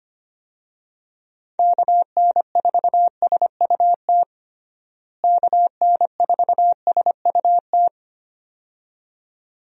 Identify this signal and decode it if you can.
Morse code, heard as a keyed tone, about 25 wpm, 700 Hz. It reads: KN4SUT KN4SUT
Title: USA Call Signs - 6 Characters - Repeated 2x - 25wpm